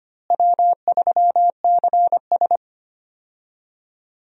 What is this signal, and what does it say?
Morse code, heard as a keyed tone, about 25 wpm, 700 Hz. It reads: W3CS
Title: USA Call Signs - 4 Characters - Repeated 2x - 25wpm